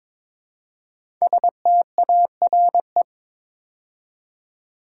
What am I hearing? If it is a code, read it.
Morse code, heard as a keyed tone, about 23 wpm, 690 Hz. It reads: STARE
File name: 5 Letter Words - First 3 Letters Given - 22wpm